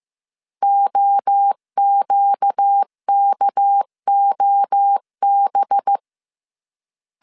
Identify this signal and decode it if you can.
Morse code, heard as a keyed tone, about 15 wpm, 800 Hz. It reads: OQKOB